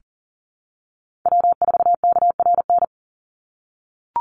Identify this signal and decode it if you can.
Morse code, heard as a keyed tone, about 40 wpm, 700 Hz. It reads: W4KRN
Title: NA SOTA Chaser Callsigns 40wpm